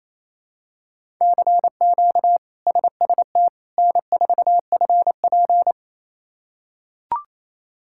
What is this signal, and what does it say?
Morse code, heard as a keyed tone, about 28 wpm, 700 Hz. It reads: CQ SST N4FP
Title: K1USN SST QSO Practice - 28wpm